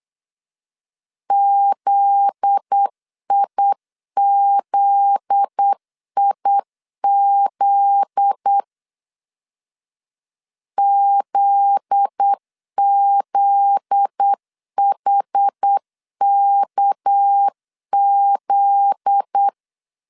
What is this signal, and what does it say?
Morse code, heard as a keyed tone, about 8 wpm, 800 Hz. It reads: ZIZIZ ZZHKZ
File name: LEZIONI / 7 GF9Z / 7 GF9Z 8wpm 3213